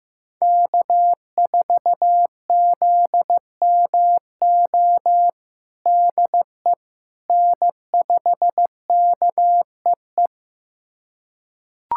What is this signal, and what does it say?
Morse code, heard as a keyed tone, about 15 wpm, 700 Hz. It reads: K4ZMO DE N5KEE
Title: USA Callsign DE Callsign - 5 Characters - 15wpm